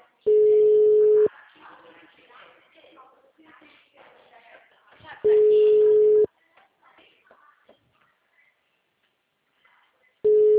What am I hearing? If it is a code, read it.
Ringback tone (Europe, 425 Hz), with speech.